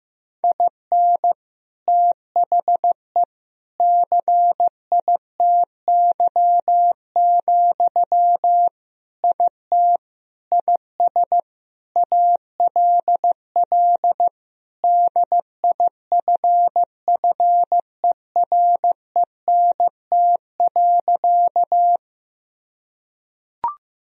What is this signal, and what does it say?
Morse code, heard as a keyed tone, about 15 wpm, 700 Hz. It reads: IN THE CITY, IT IS ALL DIFFERENT.
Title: Sentences from Top 1000 Words - Review Part 3 - 15wpm